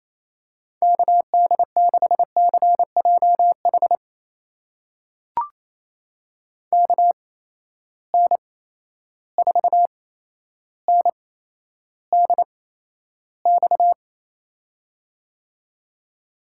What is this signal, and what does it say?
Morse code, heard as a keyed tone, about 28 wpm, spaced as for 12 wpm, 700 Hz. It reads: KD6CJH KN4NDX
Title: USA Call Signs - 8x Character Spacing - 28wpm